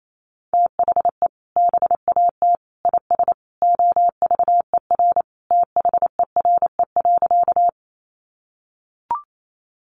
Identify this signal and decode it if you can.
Morse code, heard as a keyed tone, about 28 wpm, 700 Hz. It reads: THE BAT IS OVER THERE.
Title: Sentences from Top 1500 Words 28wpm